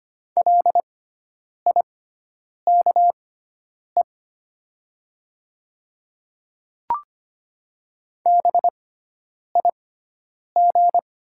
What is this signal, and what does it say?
Morse code, heard as a keyed tone, about 25 wpm, spaced as for 9 wpm, 700 Hz. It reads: LIKE BIG
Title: 6x Character Spacing - Top 300 Words - 25wpm